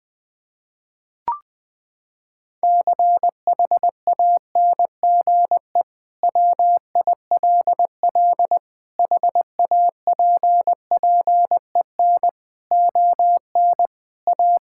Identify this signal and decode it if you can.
Morse code, heard as a keyed tone, about 21 wpm, 700 Hz. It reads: CHANGE WILL HAPPEN ON A
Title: Sentences from Top 1000 Words - Review Part 5 - 20wpm